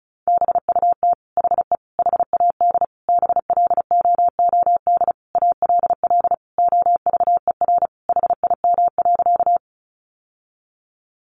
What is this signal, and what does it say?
Morse code, heard as a keyed tone, about 34 wpm, 700 Hz. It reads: BUT HE HAD BLOOD ALL OVER HIM.
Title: Sentences from Top 1100 Words 35wpm